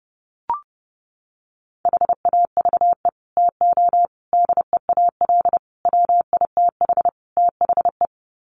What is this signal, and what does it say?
Morse code, heard as a keyed tone, about 30 wpm, 700 Hz. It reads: HAVE TO DEAL WITH THE
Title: Sets of 5 Words Encore2 30wpm